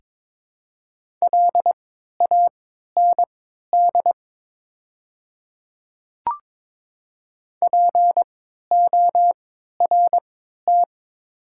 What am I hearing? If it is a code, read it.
Morse code, heard as a keyed tone, about 23 wpm, spaced as for 13 wpm, 700 Hz. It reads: LAND PORT
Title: 3x Character Spacing - Top 300 Words - 22wpm